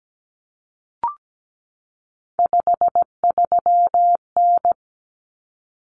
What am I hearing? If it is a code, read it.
Morse code, heard as a keyed tone, about 17 wpm, 700 Hz. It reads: 53N
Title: Common Cut Numbers - 3-Digit 17wpm